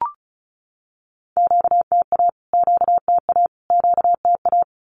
Morse code, heard as a keyed tone, about 34 wpm, 700 Hz. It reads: QTA QTA QTA
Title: Q-Codes Repeated 3x 35wpm